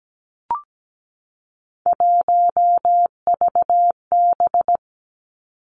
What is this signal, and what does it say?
Morse code, heard as a keyed tone, about 17 wpm, 700 Hz. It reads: 1VB
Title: Three Challenging Letter-Number Combinations 17wpm